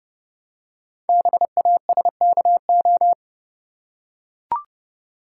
Morse code, heard as a keyed tone, about 30 wpm, 700 Hz. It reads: BASKO
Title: Code Groups - 5 Letters - 30wpm